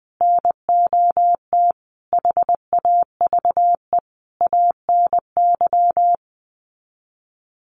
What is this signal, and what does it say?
Morse code, heard as a keyed tone, about 20 wpm, 700 Hz. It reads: NOT HAVE ANY
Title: Sets of 5 Words Encore 20wpm